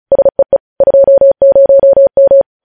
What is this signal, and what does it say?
Morse code, heard as a keyed tone, about 36 wpm, 550 Hz. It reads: SEE 20M
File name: podcast / News headlines-1x-35x35